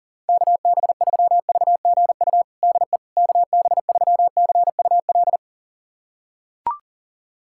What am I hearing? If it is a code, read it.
Morse code, heard as a keyed tone, about 40 wpm, 700 Hz. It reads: KB3VGU DE KB3CUL
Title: USA Callsign DE Callsign 40wpm